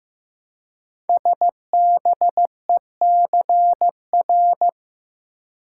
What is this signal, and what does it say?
Morse code, heard as a keyed tone, about 15 wpm, 700 Hz. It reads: SBECR
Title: Code Groups - 5 Letters - 15wpm